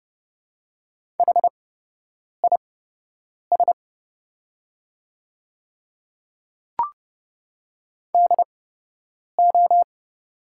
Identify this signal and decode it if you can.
Morse code, heard as a keyed tone, about 30 wpm, spaced as for 8 wpm, 700 Hz. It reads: HIS DO
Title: 8x Character Spacing - Top 100 Words - 30wpm